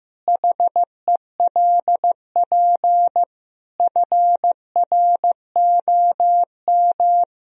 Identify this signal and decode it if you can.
Morse code, heard as a keyed tone, about 15 wpm, 700 Hz. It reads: HELP FROM
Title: Sentences from Top 1000 Words - Review Part 3 - 15wpm